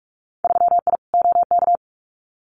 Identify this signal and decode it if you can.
Morse code, heard as a keyed tone, about 46 wpm, 700 Hz. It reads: 3I OK
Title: Winter Field Day Exchanges Encore 45wpm